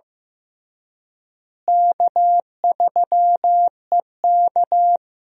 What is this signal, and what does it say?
Morse code, heard as a keyed tone, about 15 wpm, 700 Hz. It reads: K3EK
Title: USA Call Signs - 4 Characters - Repeated 3x - 15wpm